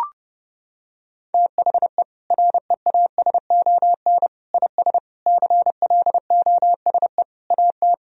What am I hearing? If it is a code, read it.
Morse code, heard as a keyed tone, about 32 wpm, 700 Hz. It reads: THE REASON IS CLOSE AT
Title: Sentences from Top 1000 Words - Review Part 1 - 30wpm